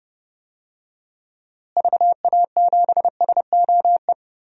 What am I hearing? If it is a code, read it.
Morse code, heard as a keyed tone, about 30 wpm, 700 Hz. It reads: VA7SOE
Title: DXCC-100 Call Signs - 6 Characters - Repeated 2x - 30wpm